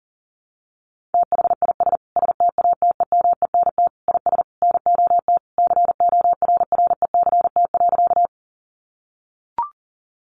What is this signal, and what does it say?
Morse code, heard as a keyed tone, about 37 wpm, 700 Hz. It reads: THIS STATEMENT IS NOT CORRECT.
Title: Sentences from Top 1200 Words 40wpm